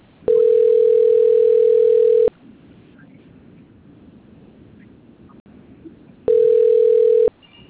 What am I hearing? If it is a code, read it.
Ringback tone (North America), 440 + 480 Hz.
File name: rg-1600-unknown- - - .34074